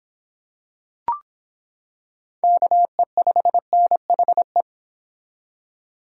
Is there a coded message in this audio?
Yes: Morse code from 2.4 to 4.6 s, heard as a keyed tone, about 25 wpm, 700 Hz. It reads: KE5NHE